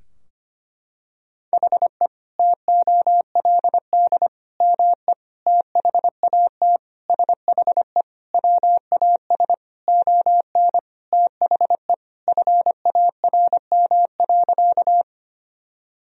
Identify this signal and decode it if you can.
Morse code, heard as a keyed tone, about 25 wpm, 700 Hz. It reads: HE TOLD ME THAT SHE WAS ON THE FARM.
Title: Sentences from Top 1000 Words - Review Part 2 - 25wpm